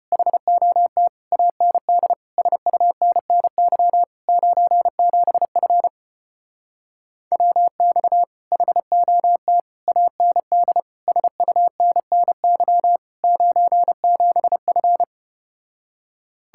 Morse code, heard as a keyed tone, about 33 wpm, 700 Hz. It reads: HOT AND SUNNY 97F WX HOT AND SUNNY 97F